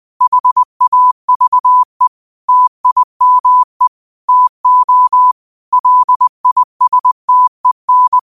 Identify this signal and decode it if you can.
Morse code, heard as a keyed tone, about 20 wpm, 1000 Hz. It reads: HAVE TIME TO LISTEN
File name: qotdfiles / 20wpm